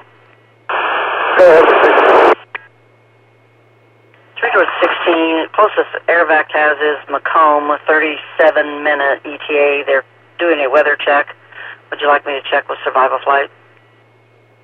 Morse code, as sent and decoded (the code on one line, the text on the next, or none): none